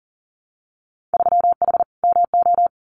Morse code, heard as a keyed tone, about 40 wpm, 700 Hz. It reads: 3H MO
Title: Winter Field Day Exchanges Encore 40wpm